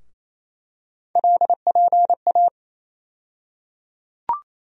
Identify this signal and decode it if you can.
Morse code, heard as a keyed tone, about 28 wpm, 700 Hz. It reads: LPA
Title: USA Call Sign Postfixes - 3 Letters 28wpm